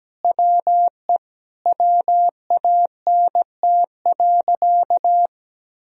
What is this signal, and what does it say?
Morse code, heard as a keyed tone, about 17 wpm, 700 Hz. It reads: WE WANT.
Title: Sentences from Top 200 Words and Abbreviations 17wpm